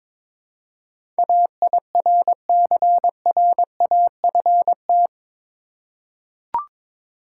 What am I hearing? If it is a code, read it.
Morse code, heard as a keyed tone, about 22 wpm, 700 Hz. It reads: AIRCRAFT
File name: practice / 22wpm / Eight Letter Words - 22wpm